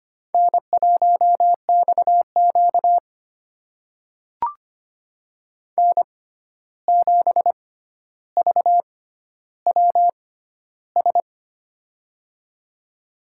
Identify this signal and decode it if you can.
Morse code, heard as a keyed tone, about 25 wpm, spaced as for 12 wpm, 700 Hz. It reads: N1XQ N7VWS
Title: USA Call Signs - 6x Character Spacing - 25wpm